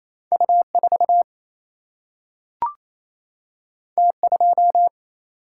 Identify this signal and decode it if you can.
Morse code, heard as a keyed tone, about 28 wpm, 700 Hz. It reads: U 4 T 2
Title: Single-Letter Single-Number 28wpm